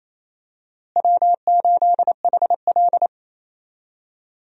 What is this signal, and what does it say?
Morse code, heard as a keyed tone, about 28 wpm, 700 Hz. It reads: W8HL
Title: USA Call Signs - 4 Characters - Repeated 2x - 28wpm